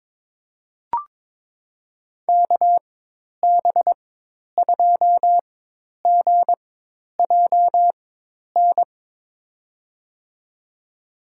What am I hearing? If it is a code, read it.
Morse code, heard as a keyed tone, about 22 wpm, spaced as for 10 wpm, 700 Hz. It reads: KB2GJN
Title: USA Call Signs - 6 Characters - 4x Character Spacing - 22wpm